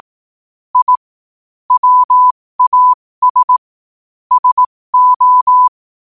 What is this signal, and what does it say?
Morse code, heard as a keyed tone, about 18 wpm, spaced as for 15 wpm, 1000 Hz. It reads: I WAS SO
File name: qotdfiles / 15wpm